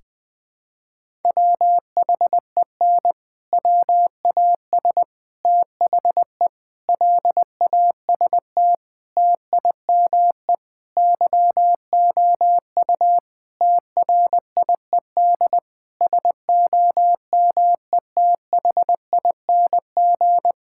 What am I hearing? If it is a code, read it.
Morse code, heard as a keyed tone, about 20 wpm, 700 Hz. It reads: WHEN WAS THE LAST TIME YOU TRIED SOMETHING
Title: Sentences with 6-Word Phrases - 20wpm